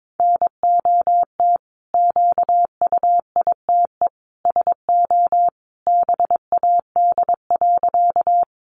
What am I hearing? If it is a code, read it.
Morse code, heard as a keyed tone, about 22 wpm, 700 Hz. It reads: NOT QUITE SO BAD.
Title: Sentences from Top 700 Words 22wpm